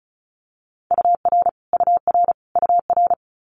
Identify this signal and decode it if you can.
Morse code, heard as a keyed tone, about 36 wpm, 700 Hz. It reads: UR UR UR
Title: Multi-letter Phonemes Repeated 3x 35wpm